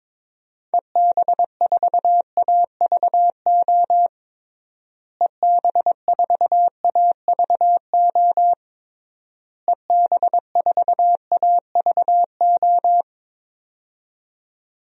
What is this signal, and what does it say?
Morse code, heard as a keyed tone, about 22 wpm, 700 Hz. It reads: EB4AVO EB4AVO EB4AVO
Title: DXCC-100 Call Signs - 6 Characters - Repeated 3x - 22wpm